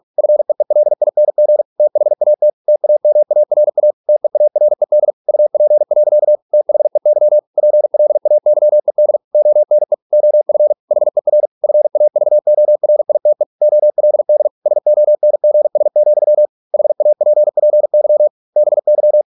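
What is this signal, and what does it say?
Morse code, heard as a keyed tone, about 48 wpm, 600 Hz. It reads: FEELING THAT TAMARA TEARED UP. THEY PLAYED ONE OF HER FAVORITE OLD SONGS, HAPPY BY
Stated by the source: Morse practice at 45 wpm, the novel Never by Ken Follett in English, from CW med Gnister.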